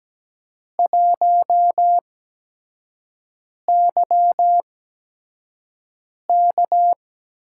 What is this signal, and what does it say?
Morse code, heard as a keyed tone, about 17 wpm, spaced as for 5 wpm, 700 Hz. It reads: 1YK
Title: USA Call Signs - 4 Characters - 8x Character Spacing - 17wpm